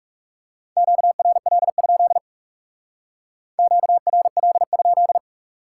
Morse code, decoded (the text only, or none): QRL? QRL?